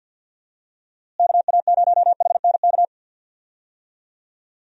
Morse code, heard as a keyed tone, about 51 wpm, 700 Hz. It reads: KA0HNK
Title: USA Call Signs - 6 Characters - Repeated 3x - 50wpm